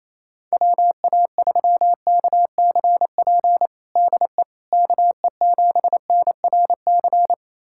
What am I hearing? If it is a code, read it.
Morse code, heard as a keyed tone, about 28 wpm, 700 Hz. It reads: WA3KCP DE KE7NRC
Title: USA Callsign DE Callsign - 6 Characters - 28wpm